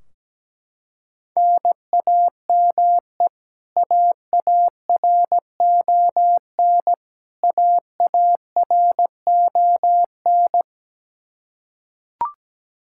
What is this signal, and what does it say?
Morse code, heard as a keyed tone, about 17 wpm, 700 Hz. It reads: NAME AARON AARON
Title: Practice QSO - Exchange 1 - Beginning - 17wpm